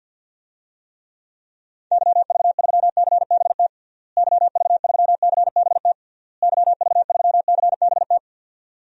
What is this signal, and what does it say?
Morse code, heard as a keyed tone, about 53 wpm, 700 Hz. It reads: YV3CBT YV3CBT YV3CBT
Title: DXCC-100 Call Signs - 6 Characters - Repeated 3x - 50wpm